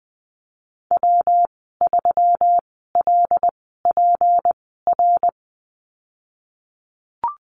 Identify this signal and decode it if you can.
Morse code, heard as a keyed tone, about 20 wpm, spaced as for 15 wpm, 700 Hz. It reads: W3LPR
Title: 173 Lesson 28g - Number 9 - Call Signs 20wpm